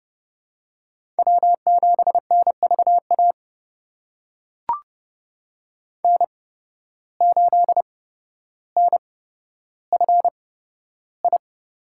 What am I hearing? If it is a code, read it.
Morse code, heard as a keyed tone, about 31 wpm, spaced as for 13 wpm, 700 Hz. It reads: W7NVA N8NFI
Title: USA Call Signs - 5 Characters - 8x Character Spacing - 30wpm